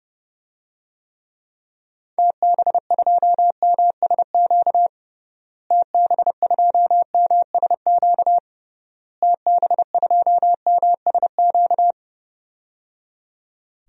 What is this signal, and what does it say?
Morse code, heard as a keyed tone, about 30 wpm, 700 Hz. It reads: TB2MSQ TB2MSQ TB2MSQ
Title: DXCC-100 Call Signs - 6 Characters - Repeated 3x - 30wpm